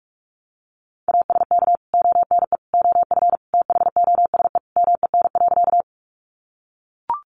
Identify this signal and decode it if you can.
Morse code, heard as a keyed tone, about 44 wpm, 700 Hz. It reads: ASK ONE OF THOSE MEN.
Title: Sentences from Top 300 Words 45wpm